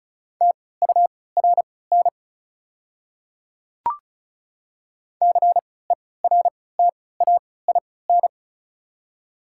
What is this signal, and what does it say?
Morse code, heard as a keyed tone, about 36 wpm, spaced as for 20 wpm, 700 Hz. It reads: TURN CERTAIN
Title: 3x Character Spacing - Top 500 Words - 35wpm